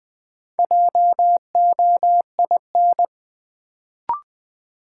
Morse code, heard as a keyed tone, about 20 wpm, 700 Hz. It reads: JOIN
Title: Sentences from Top 1400 Words - New Word Drill Alt - 20wpm